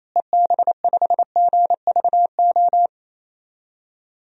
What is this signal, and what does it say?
Morse code, heard as a keyed tone, about 27 wpm, 700 Hz. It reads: EB5GVO
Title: DXCC-100 Call Signs - 6 Characters - Repeated 3x - 28wpm